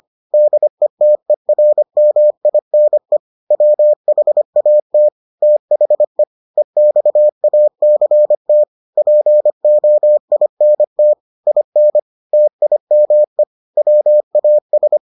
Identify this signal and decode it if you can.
Morse code, heard as a keyed tone, about 25 wpm, 600 Hz. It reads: DETERMINE WHAT THE EXACT POINT IN TIME WAS